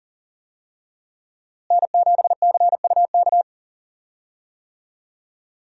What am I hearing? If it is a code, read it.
Morse code, heard as a keyed tone, about 40 wpm, 700 Hz. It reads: N7CUK
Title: USA Call Signs - 5 Characters - Repeated 3x - 40wpm